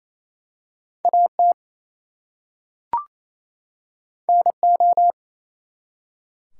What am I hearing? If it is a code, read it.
Morse code, heard as a keyed tone, about 27 wpm, 700 Hz. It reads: AT NO
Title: Two Letter Words - 28wpm